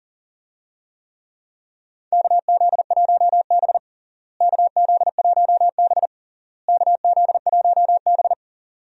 Morse code, heard as a keyed tone, about 40 wpm, 700 Hz. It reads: KZ1B KZ1B KZ1B
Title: USA Call Signs - 4 Characters - Repeated 3x - 40wpm